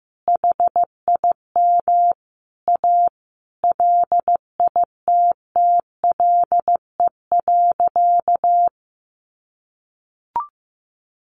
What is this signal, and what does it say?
Morse code, heard as a keyed tone, about 15 wpm, 700 Hz. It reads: HIM A LITTLE.
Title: Sentences from Top 1500 Words 15wpm